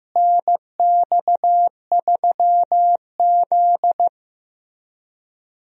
Morse code, heard as a keyed tone, about 15 wpm, 700 Hz. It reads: NX3Z
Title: USA Call Signs - 4 Characters - Repeated 3x - 15wpm